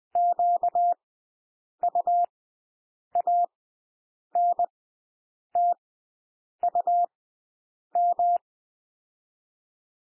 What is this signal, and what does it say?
Morse code, heard as a keyed tone, about 20 wpm, spaced as for 8 wpm, 700 Hz. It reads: QUANTUM
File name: cwmp3 / W1 SOUND WORDS 20 wpm